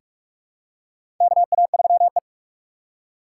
Morse code, heard as a keyed tone, about 46 wpm, 700 Hz. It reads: KA3E
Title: USA Call Signs - 4 Characters - 4x Character Spacing - 45wpm